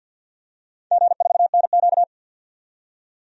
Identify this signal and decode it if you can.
Morse code, heard as a keyed tone, about 51 wpm, 700 Hz. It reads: G4NQ
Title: DXCC-100 Call Signs - 4 Characters - 50wpm